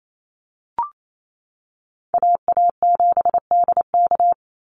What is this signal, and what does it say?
Morse code, heard as a keyed tone, about 28 wpm, 700 Hz. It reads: AA7DK
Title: NA SOTA Chaser Callsigns 28wpm